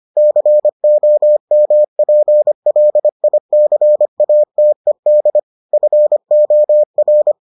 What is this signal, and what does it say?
Morse code, heard as a keyed tone, about 25 wpm, 600 Hz. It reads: COMPLICATED FOR